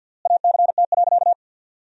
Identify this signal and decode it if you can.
Morse code, heard as a keyed tone, about 48 wpm, 700 Hz. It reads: ACT.
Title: 3x Word-Spacing - Sentences from Top 200 Words - 50wpm